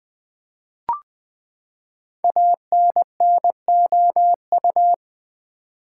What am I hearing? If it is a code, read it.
Morse code, heard as a keyed tone, about 20 wpm, 700 Hz. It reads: ANNOU
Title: Build A Word - 5 Letters At A Time - 12-Letter Words - 20wpm